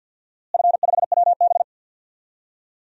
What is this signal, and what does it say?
Morse code, heard as a keyed tone, about 47 wpm, 700 Hz. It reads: F5WB
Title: DXCC-100 Call Signs - 4 Characters - Repeated 2x - 50wpm